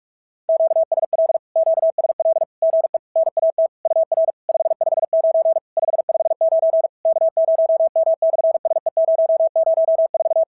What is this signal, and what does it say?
Morse code, heard as a keyed tone, about 46 wpm, 690 Hz. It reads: QSL QSL GE NAT UR 559 559 K0M/SE004
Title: SOTA QSO Practice 45wpm